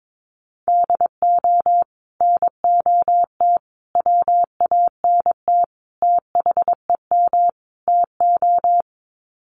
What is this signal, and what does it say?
Morse code, heard as a keyed tone, about 22 wpm, 700 Hz. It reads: DO NOT WANT THEM TO
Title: Sets of 5 Words Encore 22wpm